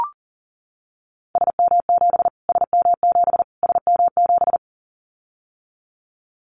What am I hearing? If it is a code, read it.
Morse code, heard as a keyed tone, about 40 wpm, 700 Hz. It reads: SM7 SM7 SM7
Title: Callsign Prefixes for Top 25 DXCC - Repeated 3x - 40wpm